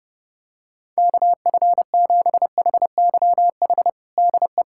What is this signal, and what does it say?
Morse code, heard as a keyed tone, about 32 wpm, 700 Hz. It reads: KF7HYH DE
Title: USA Callsign DE Callsign - 6 Characters - 30wpm